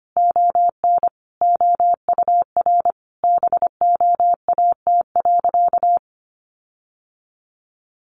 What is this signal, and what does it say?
Morse code, heard as a keyed tone, about 25 wpm, 700 Hz. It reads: ON OUR BOAT.
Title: Sentences from Top 1400 Words 25wpm